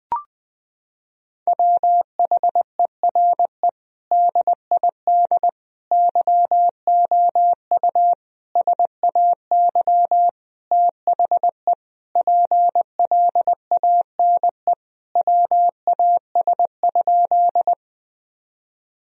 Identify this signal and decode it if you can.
Morse code, heard as a keyed tone, about 20 wpm, 700 Hz. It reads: WHERE DID YOU SAY THE PLANE WAS?
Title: Sentences from Top 1000 Words - Review Part 3 - 20wpm